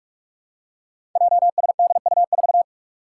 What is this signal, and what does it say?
Morse code, heard as a keyed tone, about 44 wpm, 700 Hz. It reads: JSDU4
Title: Five Challenging Letter-Number Combinations 45wpm